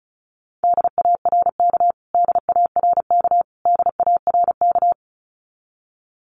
Morse code, heard as a keyed tone, about 34 wpm, 700 Hz. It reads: DARK DARK DARK
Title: Top 500 Words Repeated 35wpm